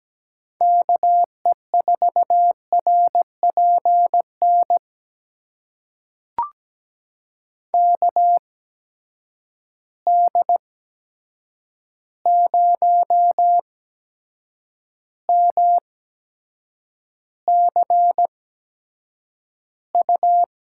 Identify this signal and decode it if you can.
Morse code, heard as a keyed tone, about 17 wpm, spaced as for 7 wpm, 700 Hz. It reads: KE4RPN KD0MCU
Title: USA Call Signs - 6 Characters - 8x Character Spacing - 17wpm